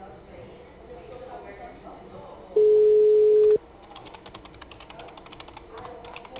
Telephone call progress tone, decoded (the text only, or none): ringback tone